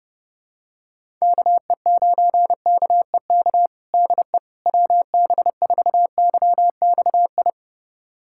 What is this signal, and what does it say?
Morse code, heard as a keyed tone, about 30 wpm, 700 Hz. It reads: KE9KEK DE WB4YXI
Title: USA Callsign DE Callsign - 6 Characters - 30wpm